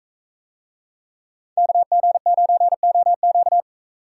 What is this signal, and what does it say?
Morse code, heard as a keyed tone, about 40 wpm, 700 Hz. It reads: KG9OQ